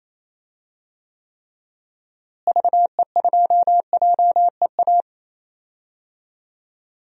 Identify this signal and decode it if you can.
Morse code, heard as a keyed tone, about 28 wpm, 700 Hz. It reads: VE2JEA
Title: DXCC-100 Call Signs - 6 Characters - Repeated 2x - 28wpm